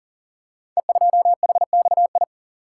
Morse code, heard as a keyed tone, about 42 wpm, 700 Hz. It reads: E2HXI
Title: Code Groups - 5 Letter:Numbers - 40wpm